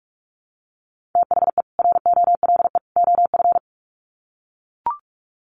Morse code, heard as a keyed tone, about 44 wpm, 700 Hz. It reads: THE ROLE OF
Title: Sets of 3 Words 45wpm